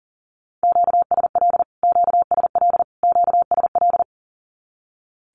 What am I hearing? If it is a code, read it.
Morse code, heard as a keyed tone, about 40 wpm, 700 Hz. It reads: QSL QSL QSL
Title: Q-Codes Repeated 3x 40wpm